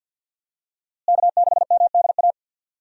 Morse code, heard as a keyed tone, about 50 wpm, 700 Hz. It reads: K6MDA
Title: USA Call Signs - 5 Characters - Repeated 3x - 50wpm